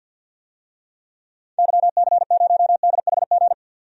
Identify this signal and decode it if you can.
Morse code, heard as a keyed tone, about 51 wpm, 700 Hz. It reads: YC0DHG
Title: DXCC-100 Call Signs - 6 Characters - 50wpm